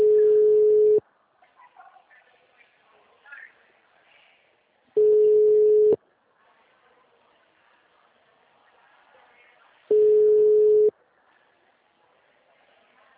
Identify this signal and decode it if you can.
Ringback tone (Europe, 425 Hz), with speech.